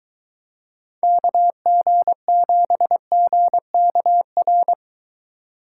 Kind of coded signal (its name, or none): Morse code